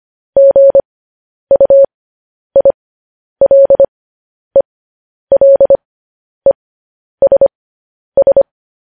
Morse code, heard as a keyed tone, about 25 wpm, spaced as for 10 wpm, 550 Hz. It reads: GUILELESS
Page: news headlines in morse code at 10 wpm